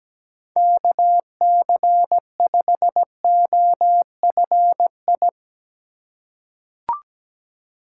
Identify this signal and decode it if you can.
Morse code, heard as a keyed tone, about 17 wpm, 700 Hz. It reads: KC5OFI
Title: USA Call Signs - 6 Characters - 2x Character Spacing - 17wpm